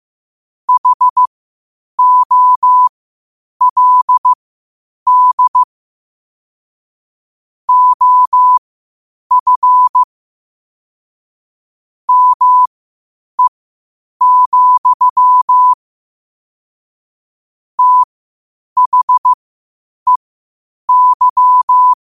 Morse code, heard as a keyed tone, about 15 wpm, spaced as for 8 wpm, 1000 Hz. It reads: HOLD OF ME, THEY
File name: qotdfiles / 8wpm